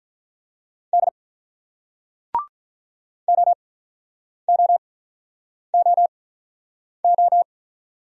Morse code, heard as a keyed tone, about 48 wpm, spaced as for 9 wpm, 700 Hz. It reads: D OOOO